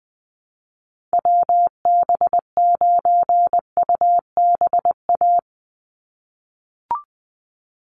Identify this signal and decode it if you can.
Morse code, heard as a keyed tone, about 20 wpm, 700 Hz. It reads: WB9UBA
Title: USA Call Signs 20wpm